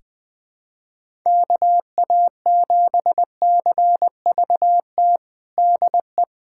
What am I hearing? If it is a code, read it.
Morse code, heard as a keyed tone, about 20 wpm, 700 Hz. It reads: KA7CVT DE
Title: USA Callsign DE Callsign 20wpm